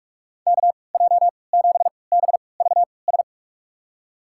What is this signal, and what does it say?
Morse code, heard as a keyed tone, about 46 wpm, spaced as for 25 wpm, 700 Hz. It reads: KJ7BVS
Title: USA Call Signs - 3x Character Spacing - 45wpm